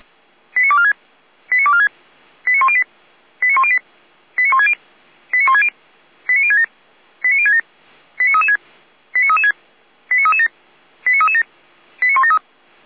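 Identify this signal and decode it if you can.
Five-tone selective call (ZVEI), heard as quick runs of five tones, tones of about 80 ms. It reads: 89246 89246 89108 89108 89160 89160 89067 89067 89336 89336 89337 89337 89173
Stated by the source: ZVEI five-tone call received via SDR.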